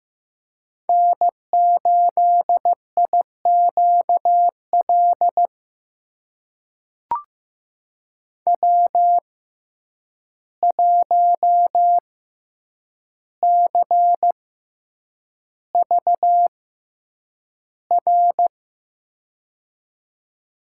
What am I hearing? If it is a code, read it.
Morse code, heard as a keyed tone, about 15 wpm, spaced as for 8 wpm, 700 Hz. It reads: N8IQL W1CVR
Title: USA Call Signs - 5 Characters - 6x Character Spacing - 15wpm